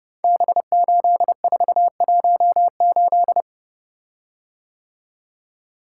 Morse code, heard as a keyed tone, about 30 wpm, 700 Hz. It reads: B8418
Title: Code Groups - 5 Letter:Numbers - 30wpm